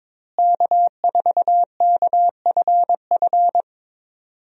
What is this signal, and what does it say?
Morse code, heard as a keyed tone, about 22 wpm, 700 Hz. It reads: K4KFF
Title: USA Call Signs - 5 Characters - Repeated 3x - 22wpm